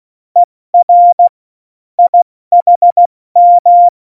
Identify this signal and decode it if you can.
Morse code, heard as a keyed tone, about 16 wpm, 700 Hz. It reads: ER IHM
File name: Morsekurs / Witze / Witz 20. Oktober 2020, 70 F80